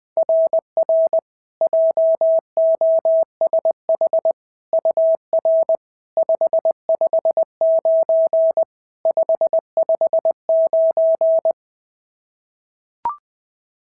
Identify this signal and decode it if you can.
Morse code, heard as a keyed tone, about 18 wpm, 650 Hz. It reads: RR JOSH UR 559 559
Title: SOTA QSO Practice 20wpm